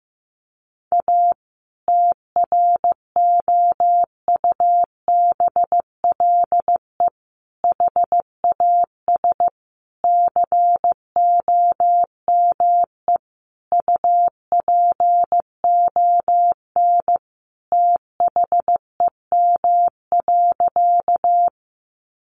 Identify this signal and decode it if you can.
Morse code, heard as a keyed tone, about 15 wpm, 700 Hz. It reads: A TROUBLE HAS COME UPON THEM.
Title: Sentences from Top 1200 Words 15wpm